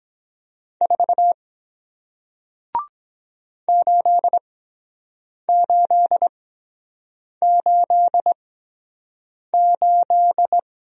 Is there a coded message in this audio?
Yes: Morse code from 0.8 to 10.6 s, heard as a keyed tone, about 23 wpm, spaced as for 7 wpm, 700 Hz. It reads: 4 8888